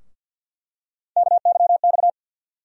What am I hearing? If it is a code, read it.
Morse code, heard as a keyed tone, about 51 wpm, 700 Hz. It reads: KYX
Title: USA Call Sign Postfixes - 3 Letters 50wpm